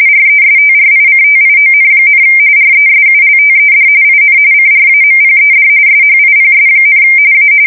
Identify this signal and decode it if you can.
RTTY, about 45.45 baud. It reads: RESTRICTION, RESULTED IN CRAMPED SHIPS, A
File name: fsk170 / USS-Massachusetts